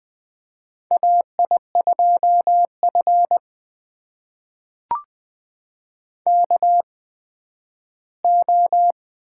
Morse code, heard as a keyed tone, about 20 wpm, spaced as for 7 wpm, 700 Hz. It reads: AI2FKO
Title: USA Call Signs - 8x Character Spacing - 20wpm